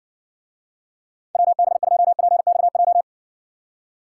Morse code, heard as a keyed tone, about 59 wpm, 700 Hz. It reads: WB2PXJ